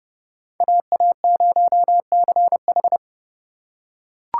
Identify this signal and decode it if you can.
Morse code, heard as a keyed tone, about 28 wpm, 700 Hz. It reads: AA0CH